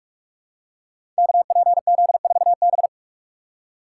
Morse code, heard as a keyed tone, about 44 wpm, 700 Hz. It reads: KPZ4B
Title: Code Groups - 5 Letter:Numbers - 45wpm